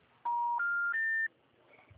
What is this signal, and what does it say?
Special information tone.
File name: out- -2005